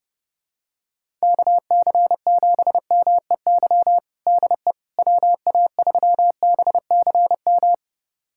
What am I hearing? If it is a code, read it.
Morse code, heard as a keyed tone, about 30 wpm, 700 Hz. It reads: KC7MEY DE WA3BCM
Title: USA Callsign DE Callsign 30wpm